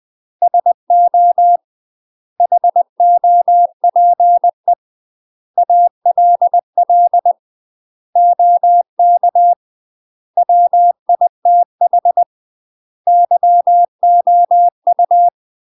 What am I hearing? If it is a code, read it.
Morse code, heard as a keyed tone, about 20 wpm, 700 Hz. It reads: SO HOPE ALL OK WITH YOU
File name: UK MASTER QSOs TRAINING / 20WPM / UK MASTER QSOs 0039